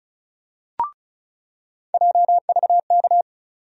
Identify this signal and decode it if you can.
Morse code, heard as a keyed tone, about 34 wpm, 700 Hz. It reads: JVK
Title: USA Call Sign Postfixes - 3 Letters 35wpm